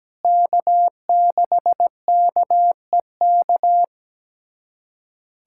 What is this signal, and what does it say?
Morse code, heard as a keyed tone, about 17 wpm, 700 Hz. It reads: K6KEK
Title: USA Callsign DE Callsign - 5 Characters - 17wpm